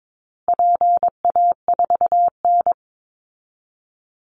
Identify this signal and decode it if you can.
Morse code, heard as a keyed tone, about 22 wpm, 700 Hz. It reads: PA4N
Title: Call Signs Repeated 2x - 22wpm